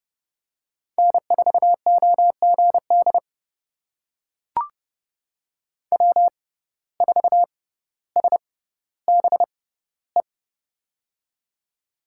Morse code, heard as a keyed tone, about 30 wpm, spaced as for 15 wpm, 700 Hz. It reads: N4OGD W4SBE
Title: USA Call Signs - 6x Character Spacing - 30wpm